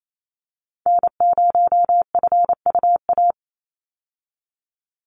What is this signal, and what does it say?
Morse code, heard as a keyed tone, about 28 wpm, 700 Hz. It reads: N0FUA
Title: USA Call Signs 28wpm